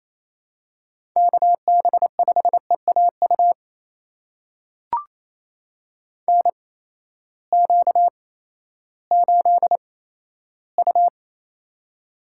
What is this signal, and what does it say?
Morse code, heard as a keyed tone, about 28 wpm, spaced as for 14 wpm, 700 Hz. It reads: KB5EAU NQ8U
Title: USA Call Signs - 8x Character Spacing - 28wpm